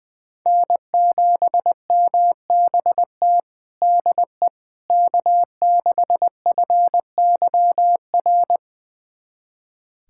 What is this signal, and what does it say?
Morse code, heard as a keyed tone, about 20 wpm, 700 Hz. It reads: N7MBT DE K6FYR
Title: USA Callsign DE Callsign - 5 Characters - 20wpm